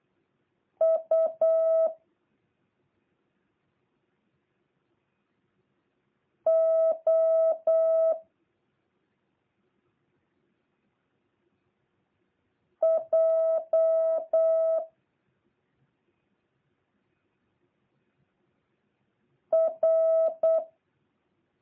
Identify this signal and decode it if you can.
Morse code, heard as a keyed tone, about 8 wpm, 650 Hz. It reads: U O J R